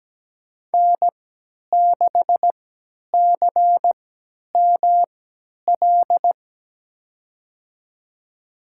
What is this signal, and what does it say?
Morse code, heard as a keyed tone, about 17 wpm, spaced as for 10 wpm, 700 Hz. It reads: N6CML
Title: USA Call Signs - 5 Characters - 3x Character Spacing - 17wpm